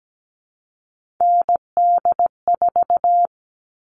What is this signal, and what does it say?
Morse code, heard as a keyed tone, about 17 wpm, 700 Hz. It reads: ND4
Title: All Cut Numbers - 3-Digit 17wpm